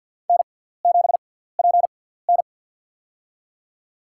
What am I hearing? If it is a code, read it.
Morse code, heard as a keyed tone, about 47 wpm, spaced as for 17 wpm, 700 Hz. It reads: N7PN
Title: USA Call Signs - 6x Character Spacing - 50wpm